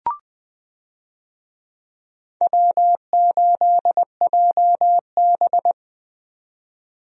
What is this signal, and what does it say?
Morse code, heard as a keyed tone, about 20 wpm, 700 Hz. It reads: W8JB
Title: USA Call Signs - 4 Characters - 20wpm